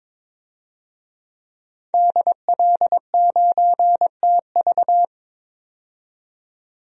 Morse code, heard as a keyed tone, about 22 wpm, 700 Hz. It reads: DL9TV
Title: DXCC-100 Call Signs - 5 Characters - 22wpm